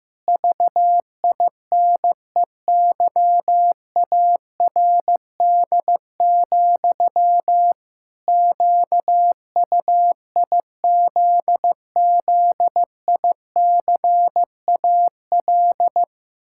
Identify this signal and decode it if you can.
Morse code, heard as a keyed tone, about 16 wpm, 700 Hz. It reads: VINEYARD, QUIZZICAL
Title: Pangram Challenge v2 - 15wpm